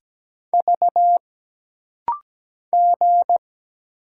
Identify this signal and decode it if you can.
Morse code, heard as a keyed tone, about 17 wpm, 700 Hz. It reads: V G
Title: Single Letter-Number Alt - Rapid-Fire 17wpm